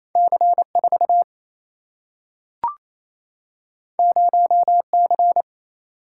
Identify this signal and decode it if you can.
Morse code, heard as a keyed tone, about 28 wpm, 700 Hz. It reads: C 4 0 C
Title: Single-Letter Single-Number 28wpm